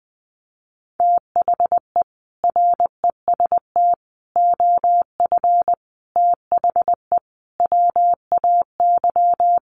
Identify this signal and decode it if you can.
Morse code, heard as a keyed tone, about 20 wpm, 700 Hz. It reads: THE REST OF THE WAY
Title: Sets of 5 Words Encore 20wpm